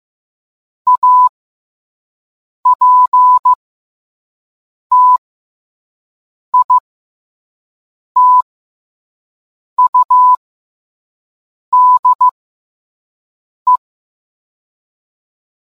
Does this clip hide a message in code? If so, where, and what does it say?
Yes: Morse code from 0.9 to 13.8 s, heard as a keyed tone, about 15 wpm, spaced as for 5 wpm, 1000 Hz. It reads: APTITUDE